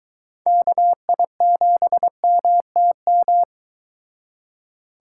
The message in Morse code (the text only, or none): KI7MTM